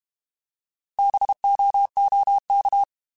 Morse code, heard as a keyed tone, about 32 wpm, 800 Hz. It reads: BOOK
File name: Alphabetical / Alphabetical mp3s / Bs @30